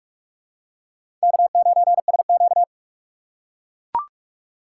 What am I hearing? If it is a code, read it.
Morse code, heard as a keyed tone, about 44 wpm, 690 Hz. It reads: K9SQ
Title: USA Call Signs - 4 Characters - 3x Character Spacing - 45wpm